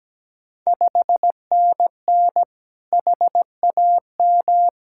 Morse code, heard as a keyed tone, about 17 wpm, 690 Hz. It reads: 5NN HAM
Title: Ontario QSO Party Exchange 17wpm